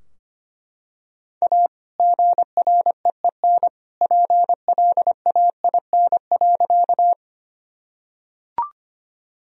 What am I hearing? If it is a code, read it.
Morse code, heard as a keyed tone, about 25 wpm, 700 Hz. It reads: A GREEN PLAIN.
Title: Sentences from Top 1000 Words - Review Part 2 - 25wpm